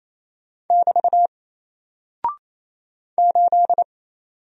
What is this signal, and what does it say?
Morse code, heard as a keyed tone, about 28 wpm, 700 Hz. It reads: <BT> 8
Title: Single Character (Letter:Number:Punctuation:Prosign) Alt 28wpm